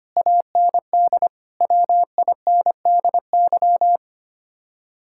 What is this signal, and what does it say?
Morse code, heard as a keyed tone, about 30 wpm, 700 Hz. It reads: AND WINDY
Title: Speed-Racing Common QSO Phrases - 25wpm